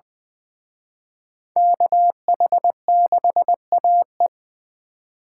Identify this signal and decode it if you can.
Morse code, heard as a keyed tone, about 24 wpm, 700 Hz. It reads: KH6AE
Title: Speed Racing - USA 2x2 Call Signs 20wpm